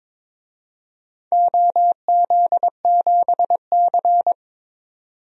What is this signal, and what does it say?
Morse code, heard as a keyed tone, about 22 wpm, 700 Hz. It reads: OZ7C
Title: DXCC-100 Call Signs - 4 Characters - 22wpm